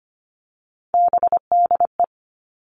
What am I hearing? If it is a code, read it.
Morse code, heard as a keyed tone, about 25 wpm, 700 Hz. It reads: BDE
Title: Three Challenging Letter-Number Combinations 25wpm